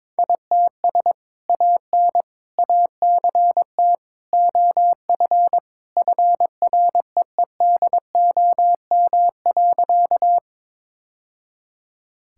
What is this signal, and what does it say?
Morse code, heard as a keyed tone, about 22 wpm, 700 Hz. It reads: ITS AN ACT OF FREEDOM.
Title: Sentences from Top 1000 Words - Review Part 5 - 22wpm